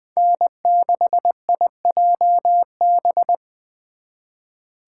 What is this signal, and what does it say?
Morse code, heard as a keyed tone, about 24 wpm, 700 Hz. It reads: N6IJB
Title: Speed Racing - USA 1x3 Call Signs 20wpm